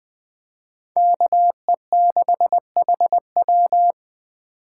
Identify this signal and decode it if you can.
Morse code, heard as a keyed tone, about 23 wpm, 700 Hz. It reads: KE6HW
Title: Speed Racing - USA 2x2 Call Signs 20wpm